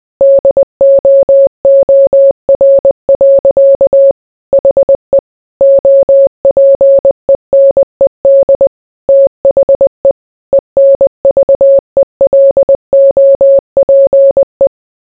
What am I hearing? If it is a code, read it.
Morse code, heard as a keyed tone, about 20 wpm, 550 Hz. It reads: DOOR. HE OPENED THE ENVELOPE